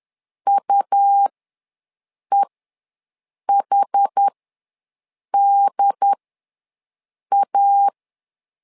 Morse code, heard as a keyed tone, about 10 wpm, spaced as for 6 wpm, 800 Hz. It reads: UEHDA
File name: LEZIONI / 2 UHDN / 2 UHD ws 10wpm 2058